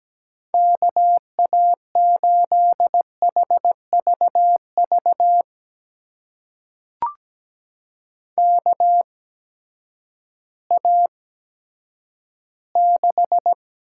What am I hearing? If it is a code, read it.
Morse code, heard as a keyed tone, about 17 wpm, spaced as for 10 wpm, 700 Hz. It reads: KA8HVV KA6
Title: USA Call Signs - 6 Characters - 8x Character Spacing - 17wpm